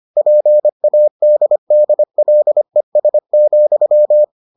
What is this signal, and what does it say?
Morse code, heard as a keyed tone, about 25 wpm, 600 Hz. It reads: PADDLES,